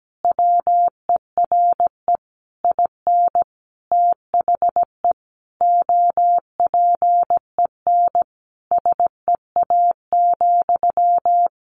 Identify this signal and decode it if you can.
Morse code, heard as a keyed tone, about 17 wpm, 700 Hz. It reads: WERE IN THE OPEN SEA,
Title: Sentences from Top 1200 Words 17wpm